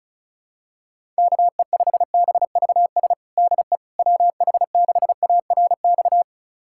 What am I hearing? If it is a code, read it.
Morse code, heard as a keyed tone, about 34 wpm, 700 Hz. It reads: KE5BVS DE WH6ARX
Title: USA Callsign DE Callsign - 6 Characters - 35wpm